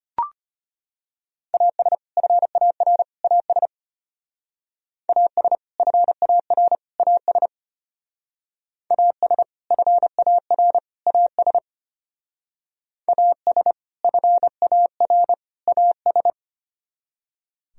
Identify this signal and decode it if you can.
Morse code, heard as a keyed tone, about 31 wpm, 700 Hz. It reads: AS FAR AS AS FAR AS AS FAR AS AS FAR AS